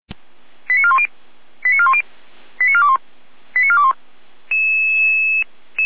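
Five-tone selective call (ZVEI), heard as quick runs of five tones, tones of about 70 ms. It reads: 89410 89410 89421 89421
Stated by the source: ZVEI five-tone call 89078 received on SDR.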